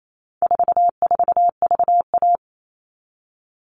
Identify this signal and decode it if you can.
Morse code, heard as a keyed tone, about 28 wpm, 700 Hz. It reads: 44VA
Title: All Cut Numbers - 4-Digit 28wpm